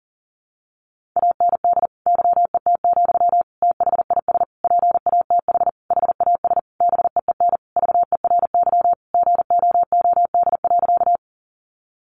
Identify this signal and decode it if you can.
Morse code, heard as a keyed tone, about 40 wpm, 700 Hz. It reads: AND YET, THIS PATH HAS BEEN VERY GOOD.
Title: Sentences from Top 1400 Words 40wpm